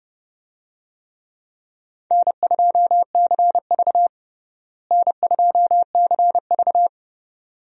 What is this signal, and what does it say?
Morse code, heard as a keyed tone, about 30 wpm, 700 Hz. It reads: N2CV N2CV
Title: USA Call Signs - 4 Characters - Repeated 2x - 30wpm